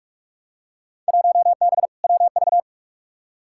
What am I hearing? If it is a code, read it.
Morse code, heard as a keyed tone, about 41 wpm, 700 Hz. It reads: 1B WV
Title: Field Day Exchanges - 45wpm